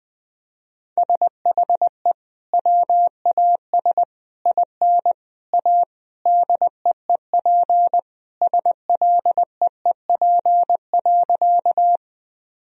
Morse code, heard as a keyed tone, about 20 wpm, 700 Hz. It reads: SHE WAS IN A DEEP SLEEP.
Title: Sentences from Top 1000 Words - Review Part 3 - 20wpm